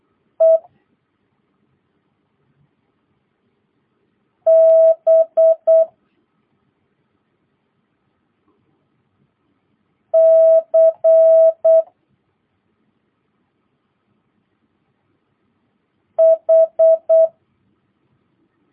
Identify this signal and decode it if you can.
Morse code, heard as a keyed tone, about 8 wpm, 650 Hz. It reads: E B C H
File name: jh8cbh / 2022cw